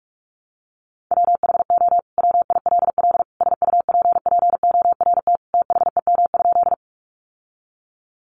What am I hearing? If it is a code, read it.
Morse code, heard as a keyed tone, about 44 wpm, 700 Hz. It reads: WHO WILL SUPPORT THEM?
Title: Sentences from Top 800 Words 45wpm